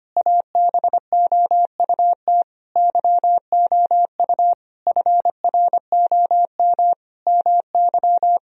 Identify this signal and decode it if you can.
Morse code, heard as a keyed tone, about 25 wpm, 700 Hz. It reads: ABOUT YOU FROM MY
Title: Sentences from Top 1000 Words - Review Part 1 - 25wpm